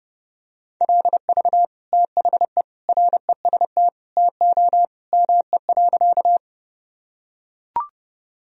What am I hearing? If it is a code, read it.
Morse code, heard as a keyed tone, about 30 wpm, 700 Hz. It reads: LV THE REST TO ME.
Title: Sentences from Top 500 Words and Abbreviations 30wpm